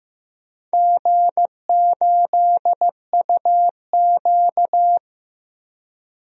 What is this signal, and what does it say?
Morse code, heard as a keyed tone, about 15 wpm, 700 Hz. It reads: G8UQ
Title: DXCC-100 Call Signs - 4 Characters - Repeated 2x - 15wpm